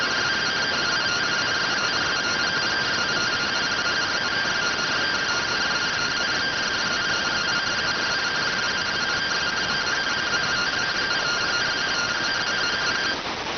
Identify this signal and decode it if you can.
Two-tone teleprinter signal, about 170 Hz shift.